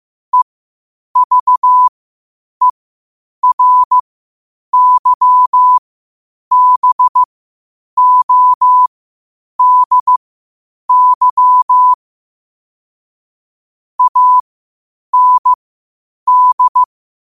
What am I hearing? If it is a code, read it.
Morse code, heard as a keyed tone, about 15 wpm, spaced as for 8 wpm, 1000 Hz. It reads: EVERYBODY AND
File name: qotdfiles / 8wpm